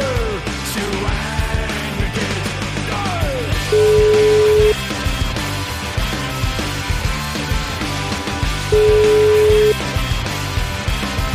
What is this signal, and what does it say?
Ringback tone (Europe, 425 Hz), with noise.